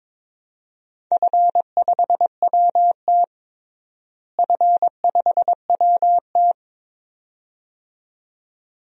Morse code, heard as a keyed tone, about 22 wpm, 700 Hz. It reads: F5WT F5WT
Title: DXCC-100 Call Signs - 4 Characters - Repeated 2x - 22wpm